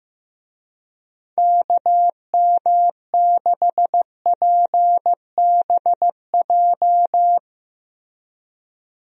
Morse code, heard as a keyed tone, about 15 wpm, 700 Hz. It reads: KM6PBJ
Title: USA Call Signs - 6 Characters - Repeated 2x - 15wpm